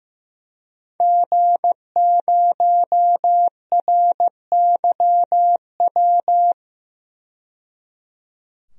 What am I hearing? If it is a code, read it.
Morse code, heard as a keyed tone, about 16 wpm, 700 Hz. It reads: G0RYW
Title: DXCC-100 Call Signs - 5 Characters - 15wpm